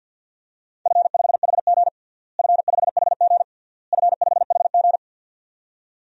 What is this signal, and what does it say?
Morse code, heard as a keyed tone, about 53 wpm, 700 Hz. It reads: F5HG F5HG F5HG
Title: DXCC-100 Call Signs - 4 Characters - Repeated 3x - 50wpm